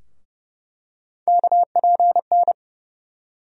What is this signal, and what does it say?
Morse code, heard as a keyed tone, about 30 wpm, 700 Hz. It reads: KPN
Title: USA Call Sign Postfixes - 3 Letters 30wpm